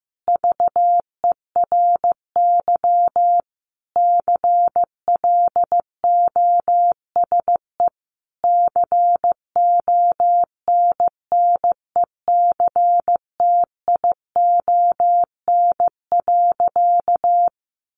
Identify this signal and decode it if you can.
Morse code, heard as a keyed tone, about 15 wpm, 700 Hz. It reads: VERY CLOSE CONNECTION.
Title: Sentences from Top 1300 Words 15wpm